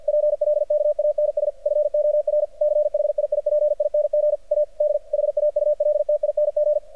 Morse code, heard as a keyed tone, about 51 wpm, 600 Hz. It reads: PLKANS FOR CHIIPING A D HAULTING
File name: archived / CRSnet